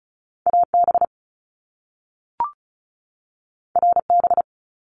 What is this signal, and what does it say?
Morse code, heard as a keyed tone, about 36 wpm, 700 Hz. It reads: AB RB
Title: Top Two Letter Combinations in Words 35wpm